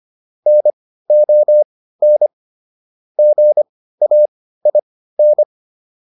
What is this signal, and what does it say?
Morse code, heard as a keyed tone, about 25 wpm, spaced as for 15 wpm, 600 Hz. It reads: NON GAIN